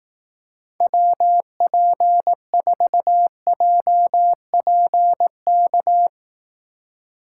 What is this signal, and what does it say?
Morse code, heard as a keyed tone, about 18 wpm, 700 Hz. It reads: WP4JPK
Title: Speed Racing - USA 2x3 Call Signs 15wpm